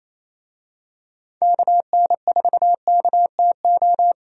Morse code, heard as a keyed tone, about 28 wpm, 700 Hz. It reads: KN4KTO
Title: USA Call Signs - 6 Characters - Repeated 3x - 28wpm